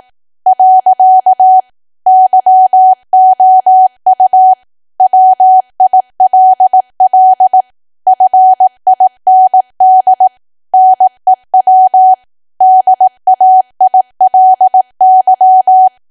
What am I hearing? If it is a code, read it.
Morse code, heard as a keyed tone, about 18 wpm, 750 Hz. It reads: . YOU WILL FIND NEW DAILY
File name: cwa / W1AW-Level2 / 18WPM